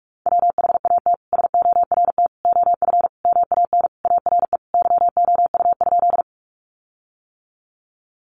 Morse code, heard as a keyed tone, about 44 wpm, 700 Hz. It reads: WHAT SORT OF MAN ARE YOU?
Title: Sentences from Top 1300 Words 45wpm